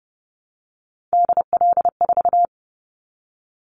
Morse code, heard as a keyed tone, about 30 wpm, 700 Hz. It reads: DL4
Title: Callsign Prefixes for Top 25 DXCC - 30wpm